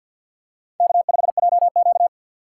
Morse code, heard as a keyed tone, about 49 wpm, 700 Hz. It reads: K5JQ
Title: USA Call Signs - 4 Characters - 3x Character Spacing - 50wpm